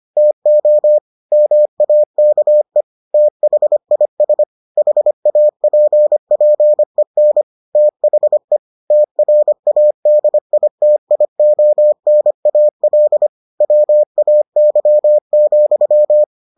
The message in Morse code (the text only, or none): TO MAKE THIS HAPPEN THE TRADITIONAL WAY,